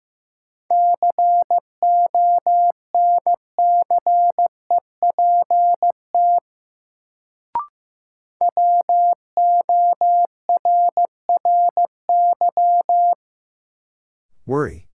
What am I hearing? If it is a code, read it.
Morse code, heard as a keyed tone, about 15 wpm, 700 Hz. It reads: CONCEPT WORRY
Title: Sentences from Top 1100 Words - New Word Drill - 15wpm